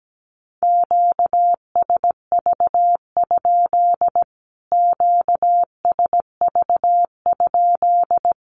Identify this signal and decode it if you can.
Morse code, heard as a keyed tone, about 17 wpm, 700 Hz. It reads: QSV? QSV?
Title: Q-Codes Repeated 3x 17wpm